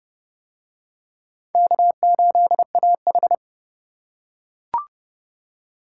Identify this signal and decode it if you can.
Morse code, heard as a keyed tone, about 27 wpm, 700 Hz. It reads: K8AH